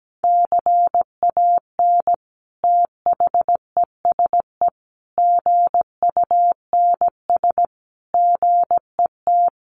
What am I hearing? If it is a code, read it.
Morse code, heard as a keyed tone, about 17 wpm, 700 Hz. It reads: CAN THESE GUNS GET
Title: A Sound Of Thunder 17wpm